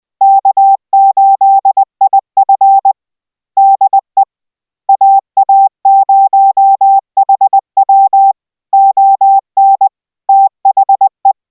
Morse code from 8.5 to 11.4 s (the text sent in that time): ON THE